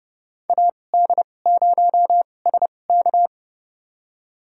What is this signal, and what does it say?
Morse code, heard as a keyed tone, about 32 wpm, spaced as for 22 wpm, 700 Hz. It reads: AD0SK
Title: USA Call Signs - 5 Characters - 2x Character Spacing - 30wpm